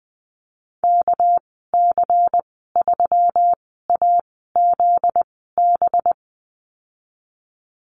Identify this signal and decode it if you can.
Morse code, heard as a keyed tone, about 20 wpm, spaced as for 14 wpm, 700 Hz. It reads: KC3AZB
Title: 245 Lesson 39g - Letter z - Call Signs 20wpm